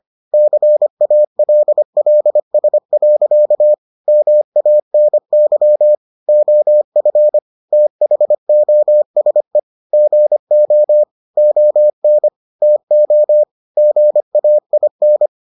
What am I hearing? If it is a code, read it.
Morse code, heard as a keyed tone, about 25 wpm, 600 Hz. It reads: CALLS. MANY OF THOSE GO ON TO GAIN